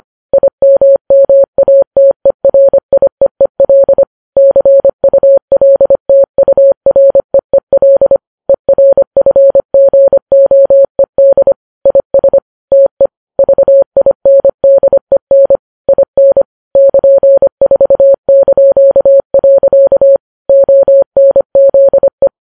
Morse code, heard as a keyed tone, about 25 wpm, 550 Hz. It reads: IMMATERIEEL CULTUREEL ERFGOED IS TE VINDEN IN <KN>4). ONZE